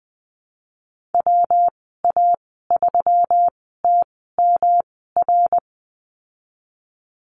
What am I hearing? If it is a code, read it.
Morse code, heard as a keyed tone, about 19 wpm, spaced as for 14 wpm, 700 Hz. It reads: WA3TMR
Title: 154 Lesson 25g - Letter p - Call Signs 20wpm